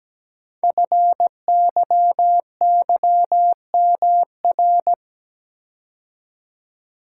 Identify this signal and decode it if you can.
Morse code, heard as a keyed tone, about 17 wpm, 700 Hz. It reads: FYYMR
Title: Code Groups - 5 Letter:Numbers - 17wpm